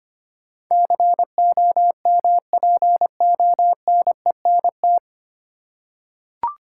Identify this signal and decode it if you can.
Morse code, heard as a keyed tone, about 25 wpm, 700 Hz. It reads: COMPONENT
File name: practice / 25wpm / 9 Letter Words - First 3 Letters Given - 25wpm